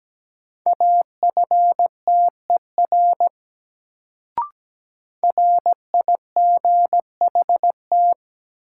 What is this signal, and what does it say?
Morse code, heard as a keyed tone, about 17 wpm, 700 Hz. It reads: AFTER RIGHT
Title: Sentences from Top 200 Words - New Word Drill - 17wpm